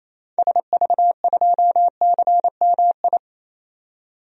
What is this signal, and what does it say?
Morse code, heard as a keyed tone, about 28 wpm, 700 Hz. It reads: SV2CMI
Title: DXCC-100 Call Signs - 6 Characters - 28wpm